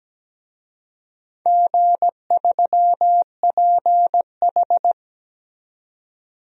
Morse code, heard as a keyed tone, about 17 wpm, 700 Hz. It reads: G3PH
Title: DXCC-100 Call Signs - 4 Characters - 17wpm